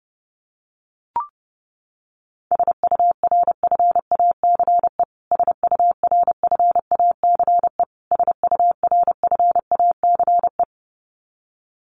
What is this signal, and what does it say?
Morse code, heard as a keyed tone, about 30 wpm, 700 Hz. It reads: SURFACE SURFACE SURFACE
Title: Top 500 Words Repeated 30wpm